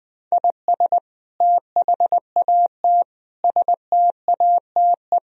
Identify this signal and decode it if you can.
Morse code, heard as a keyed tone, about 21 wpm, 700 Hz. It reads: IS THAT STATE
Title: Sentences from Top 1000 Words - Review Part 3 - 20wpm